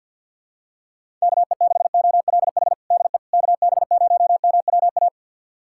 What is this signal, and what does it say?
Morse code, heard as a keyed tone, about 52 wpm, 700 Hz. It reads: KE6OLH DE KB0MWA
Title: USA Callsign DE Callsign - 6 Characters - 50wpm